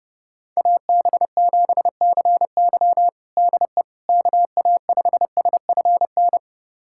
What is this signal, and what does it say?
Morse code, heard as a keyed tone, about 30 wpm, 700 Hz. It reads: AB7CY DE KA5SFN
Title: USA Callsign DE Callsign 30wpm